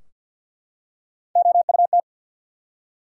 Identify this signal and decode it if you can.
Morse code, heard as a keyed tone, about 53 wpm, 700 Hz. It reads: OUT
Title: Three Letter Words - 50wpm